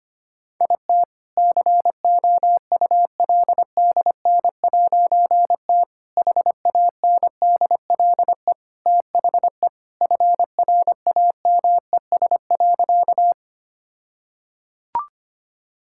Morse code, heard as a keyed tone, about 25 wpm, 700 Hz. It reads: IT COULDN'T HANDLE THE FRAMES.
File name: practice / 25wpm / All-Age Appropriate Jokes - 25wpm